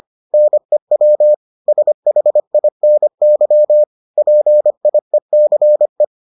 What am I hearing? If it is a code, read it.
Morse code, heard as a keyed tone, about 25 wpm, 600 Hz. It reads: NEW SHINY PIECE